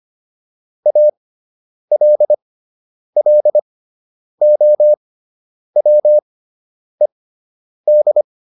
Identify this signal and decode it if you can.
Morse code, heard as a keyed tone, about 25 wpm, spaced as for 9 wpm, 600 Hz. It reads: ALLOWED